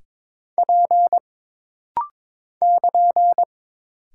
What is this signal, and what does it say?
Morse code, heard as a keyed tone, about 22 wpm, 700 Hz. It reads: P <KN>
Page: This rapid-fire timing is geared to push your ICR (Instant Character Recognition) skills.